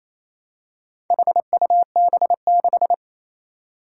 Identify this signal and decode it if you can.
Morse code, heard as a keyed tone, about 28 wpm, 700 Hz. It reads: HUB6
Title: Four Challenging Letter-Number Combinations 28wpm